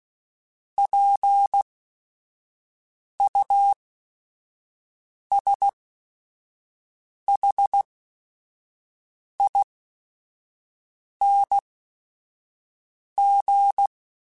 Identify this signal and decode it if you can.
Morse code, heard as a keyed tone, about 16 wpm, spaced as for 5 wpm, 800 Hz. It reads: PUSHING